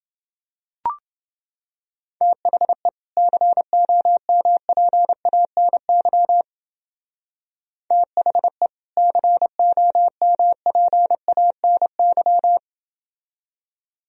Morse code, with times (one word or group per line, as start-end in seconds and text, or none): 2.2-2.9 THE
3.2-6.4 COMPANY
7.9-8.7 THE
9.0-12.6 COMPANY